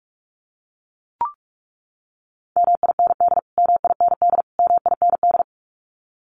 Morse code, heard as a keyed tone, about 44 wpm, 700 Hz. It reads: MIND MIND MIND
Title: Top 500 Words Repeated 45wpm